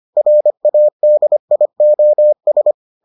Morse code, heard as a keyed tone, about 25 wpm, 600 Hz. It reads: RADIOS